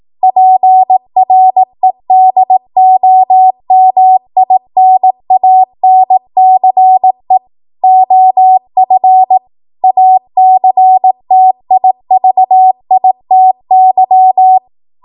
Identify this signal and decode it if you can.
Morse code, heard as a keyed tone, about 18 wpm, 750 Hz. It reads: PREDOMINANCE OF ACTIVITY
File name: Morse / 18WPM